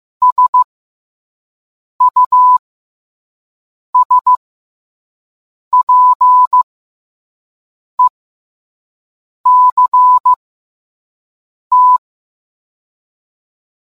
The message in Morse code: SUSPECT

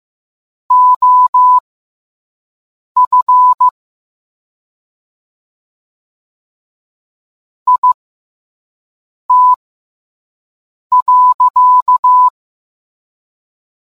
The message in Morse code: OF IT.

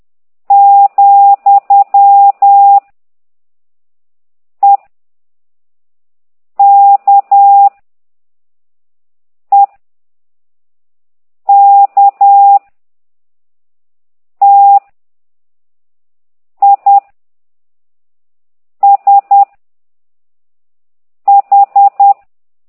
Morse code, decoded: , E K E K T I S H

